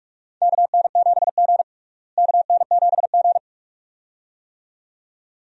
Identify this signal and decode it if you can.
Morse code, heard as a keyed tone, about 46 wpm, 700 Hz. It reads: KN7G KN7G
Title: USA Call Signs - 4 Characters - Repeated 2x - 45wpm